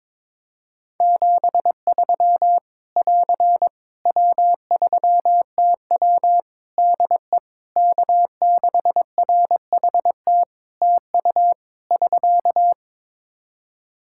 Morse code, heard as a keyed tone, about 22 wpm, 700 Hz. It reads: 73 <AR> W3TW DE K6RHT TU <SK>
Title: Practice QSO - Exchange 4 - Ending - 22wpm